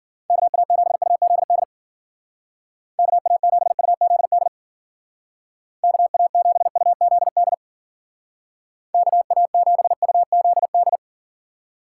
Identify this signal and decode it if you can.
Morse code, heard as a keyed tone, about 48 wpm, 700 Hz. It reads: KA7UZD KA7UZD KA7UZD KA7UZD
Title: Speed Racing - USA 2x3 Call Signs 40wpm